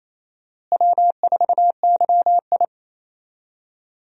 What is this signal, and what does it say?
Morse code, heard as a keyed tone, about 28 wpm, 700 Hz. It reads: W4YI
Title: USA Call Signs - 4 Characters - Repeated 2x - 28wpm